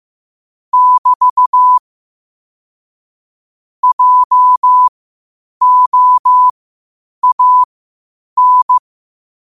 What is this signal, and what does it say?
Morse code, heard as a keyed tone, about 15 wpm, spaced as for 8 wpm, 1000 Hz. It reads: <BT> JOAN
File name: qotdfiles / 8wpm